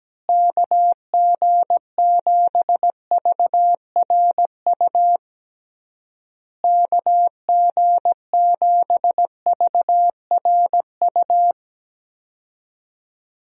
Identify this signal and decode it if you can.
Morse code, heard as a keyed tone, about 17 wpm, 700 Hz. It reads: KG7VRU KG7VRU
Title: USA Call Signs - 6 Characters - Repeated 3x - 17wpm